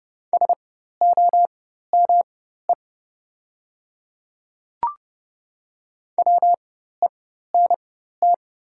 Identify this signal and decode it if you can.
Morse code, heard as a keyed tone, about 30 wpm, spaced as for 14 wpm, 700 Hz. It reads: SOME WENT
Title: 4x Character Spacing - Top 200 Words - 30wpm